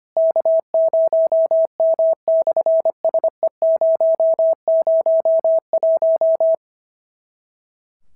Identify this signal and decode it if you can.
Morse code, heard as a keyed tone, about 25 wpm, 650 Hz. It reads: K0M/SE001
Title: SOTA QSO Practice 25wpm